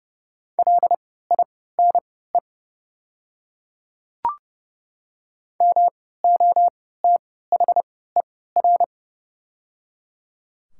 Morse code, heard as a keyed tone, about 30 wpm, spaced as for 17 wpm, 700 Hz. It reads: LINE MOTHER
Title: 3x Character Spacing - Top 200 Words - 30wpm